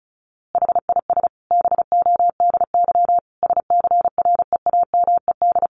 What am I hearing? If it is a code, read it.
Morse code, heard as a keyed tone, about 34 wpm, 700 Hz. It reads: HIS BODY SCREAMED
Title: A Sound Of Thunder 35wpm